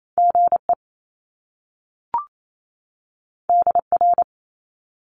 Morse code, heard as a keyed tone, about 28 wpm, 700 Hz. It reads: GE DR
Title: Top Two Letter Combinations in Words 28wpm